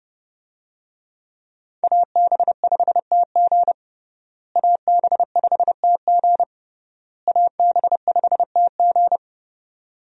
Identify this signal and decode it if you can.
Morse code, heard as a keyed tone, about 30 wpm, 700 Hz. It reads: AB5TG AB5TG AB5TG
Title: USA Call Signs - 5 Characters - Repeated 3x - 30wpm